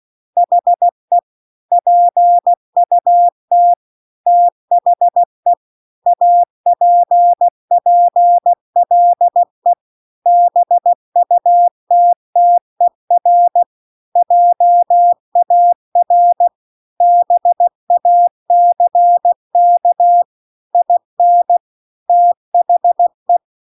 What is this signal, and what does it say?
Morse code, heard as a keyed tone, about 16 wpm, 700 Hz. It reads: HE PUT THE APPLE BUTTER JAR BACK IN THE
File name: CW-The-Big-Twoheaded-River-2-16WPM